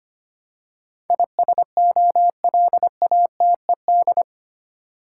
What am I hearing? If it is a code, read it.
Morse code, heard as a keyed tone, about 25 wpm, 700 Hz. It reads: ISOLATED
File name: 8 Letter Words - First 3 Letters Given - 25wpm